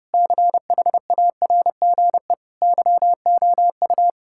Morse code, heard as a keyed tone, about 29 wpm, 700 Hz. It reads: CHARGE YOU
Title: Sentences from Top 1000 Words - Review Part 6 - 30wpm